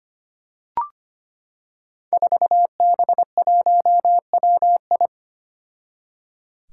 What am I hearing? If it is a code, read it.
Morse code, heard as a keyed tone, about 25 wpm, 700 Hz. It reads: 4B1WI
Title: Five Challenging Letter-Number Combinations 25wpm